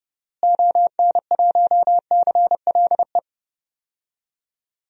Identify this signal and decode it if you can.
Morse code, heard as a keyed tone, about 30 wpm, 700 Hz. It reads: ON1CLE
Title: DXCC-100 Call Signs - 6 Characters - Repeated 2x - 30wpm